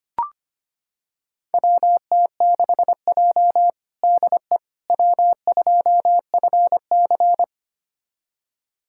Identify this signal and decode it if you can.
Morse code, heard as a keyed tone, about 25 wpm, 700 Hz. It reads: WT6J DE W2FC
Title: USA Callsign DE Callsign - 4 Characters - 25wpm